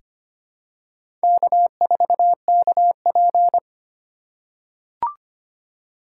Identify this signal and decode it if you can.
Morse code, heard as a keyed tone, about 25 wpm, 700 Hz. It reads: K4KP
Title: USA Call Signs - 4x Character Spacing - 25wpm